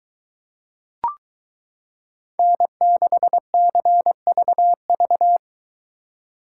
Morse code, heard as a keyed tone, about 23 wpm, 700 Hz. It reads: N6CVV